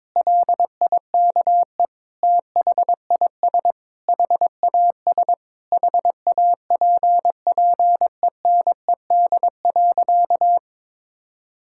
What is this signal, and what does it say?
Morse code, heard as a keyed tone, about 22 wpm, 700 Hz. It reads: LIKE THIS HAS HAPPENED.
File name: practice / 22wpm / Sentences with 8-Word Phrases - 22wpm